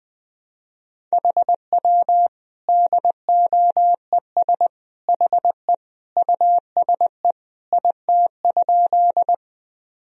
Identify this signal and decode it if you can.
Morse code, heard as a keyed tone, about 20 wpm, 700 Hz. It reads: HW DOES HE USE IT?
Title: Sentences from Top 200 Words and Abbreviations 20wpm